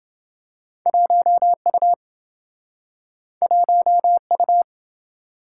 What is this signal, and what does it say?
Morse code, heard as a keyed tone, about 27 wpm, 700 Hz. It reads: 1U 1U